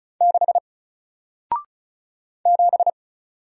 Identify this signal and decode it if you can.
Morse code, heard as a keyed tone, about 34 wpm, 700 Hz. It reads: 6 7
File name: Single-Letter-Number-35wpm